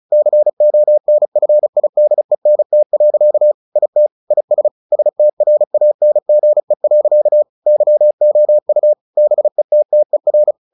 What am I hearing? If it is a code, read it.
Morse code, heard as a keyed tone, about 34 wpm, 600 Hz. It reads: CONFIDENT. IT IS STRANGE. YOU BETTER
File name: the old man and the sea 2 35wpm